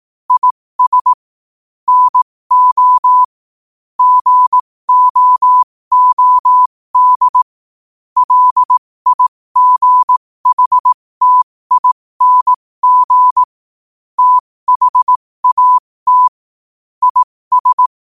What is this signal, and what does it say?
Morse code, heard as a keyed tone, about 18 wpm, spaced as for 15 wpm, 1000 Hz. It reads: IS NO GOOD LIGHTING THAT IS
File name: qotdfiles / 15wpm